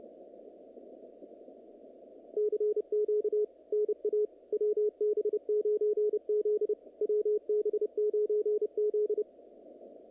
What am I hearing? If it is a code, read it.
Morse code, heard as a keyed tone, about 29 wpm, 440 Hz, with radio noise. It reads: CQ NA WB9Z WB9Z
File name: contests / naqp-cw-k5tr-op-k5pi- -2200